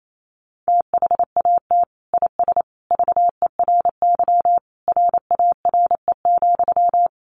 Morse code, heard as a keyed tone, about 28 wpm, 700 Hz. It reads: THAT IS VERY RARE,
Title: Sentences from Top 1100 Words 28wpm